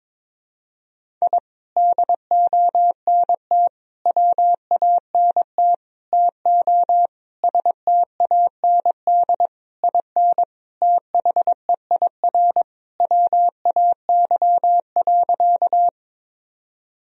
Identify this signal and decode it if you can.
Morse code, heard as a keyed tone, about 22 wpm, 700 Hz. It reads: I DONT WANT TO STAND IN THEIR WAY.
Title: Sentences from Top 1000 Words - Review Part 1 - 22wpm